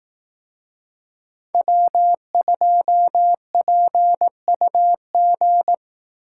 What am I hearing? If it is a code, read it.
Morse code, heard as a keyed tone, about 18 wpm, 700 Hz. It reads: W2PUG
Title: Speed Racing - USA Call Signs 15wpm